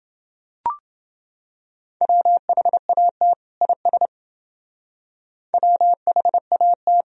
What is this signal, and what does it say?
Morse code, heard as a keyed tone, about 28 wpm, 700 Hz. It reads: WHAT IS WHAT